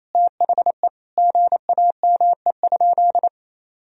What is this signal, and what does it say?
Morse code, heard as a keyed tone, about 28 wpm, 700 Hz. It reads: THE GAME?
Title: Sentences from Top 1000 Words - Review Part 4 - 28wpm